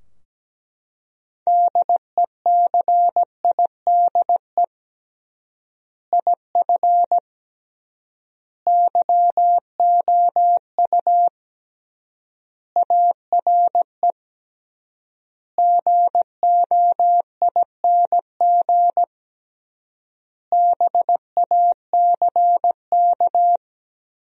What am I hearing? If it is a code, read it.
Morse code, heard as a keyed tone, about 17 wpm, 700 Hz. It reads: DECIDE IF YOU ARE GOING BACK
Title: 3x Word-Spacing - Sentences from Top 500 Words - 17wpm